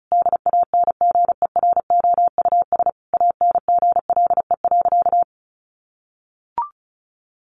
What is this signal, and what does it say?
Morse code, heard as a keyed tone, about 34 wpm, 700 Hz. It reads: DANGEROUS ANGLE.
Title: Sentences from Top 1500 Words 35wpm